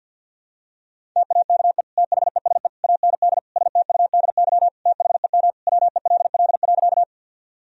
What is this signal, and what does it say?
Morse code, heard as a keyed tone, about 51 wpm, 700 Hz. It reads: TAKE THESE AND STUDY THEM WELL.
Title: Sentences from Top 1000 Words - Review Part 1 - 50wpm